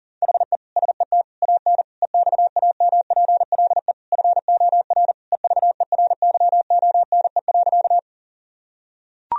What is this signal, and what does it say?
Morse code, heard as a keyed tone, about 40 wpm, 700 Hz. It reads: HE SET AN EXAMPLE FOR EVERYONE.
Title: Sentences from Top 1000 Words - Review Part 1 - 40wpm